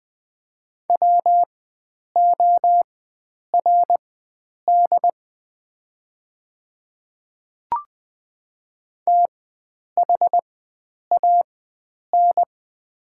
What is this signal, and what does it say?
Morse code, heard as a keyed tone, about 20 wpm, spaced as for 9 wpm, 700 Hz. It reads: WORD THAN
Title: 4x Character Spacing - Top 100 Words - 20wpm